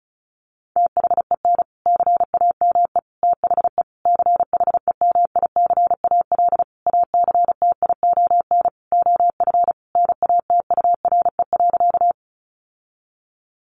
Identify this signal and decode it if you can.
Morse code, heard as a keyed tone, about 34 wpm, 700 Hz. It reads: THEN CAME THE CHEMICAL ACTION OF NATURE.
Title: Sentences from Top 1200 Words 35wpm